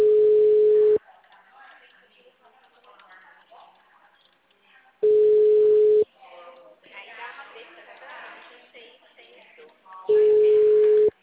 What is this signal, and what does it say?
Ringback tone (Europe, 425 Hz), with speech.